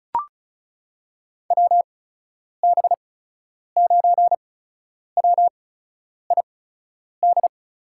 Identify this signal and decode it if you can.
Morse code, heard as a keyed tone, about 36 wpm, spaced as for 10 wpm, 700 Hz. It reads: WB9WID
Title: USA Call Signs - 6 Characters - 8x Character Spacing - 35wpm